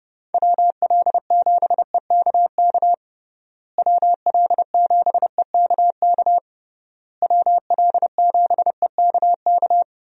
Morse code, heard as a keyed tone, about 30 wpm, 700 Hz. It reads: WL7EKK WL7EKK WL7EKK
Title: USA Call Signs - 6 Characters - Repeated 3x - 30wpm